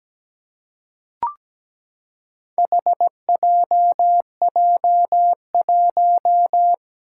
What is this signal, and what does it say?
Morse code, heard as a keyed tone, about 17 wpm, 700 Hz. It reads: HJJ1
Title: Four Challenging Letter-Number Combinations 17wpm